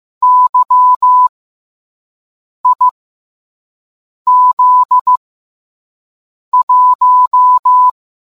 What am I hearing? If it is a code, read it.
Morse code, heard as a keyed tone, about 15 wpm, 1000 Hz. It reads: Y I Z 1